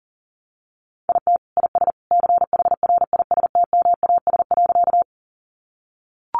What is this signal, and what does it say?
Morse code, heard as a keyed tone, about 40 wpm, 700 Hz. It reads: IT IS CHRISTMAS.
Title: Easy Phrases 40wpm